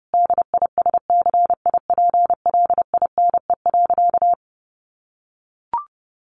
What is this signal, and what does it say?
Morse code, heard as a keyed tone, about 30 wpm, 700 Hz. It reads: DISCIPLINE.
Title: Sentences from Top 900 Words 30wpm